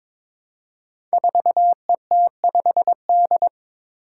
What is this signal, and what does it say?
Morse code, heard as a keyed tone, about 22 wpm, 700 Hz. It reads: 4ET5D
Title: Five Challenging Letter-Number Combinations 22wpm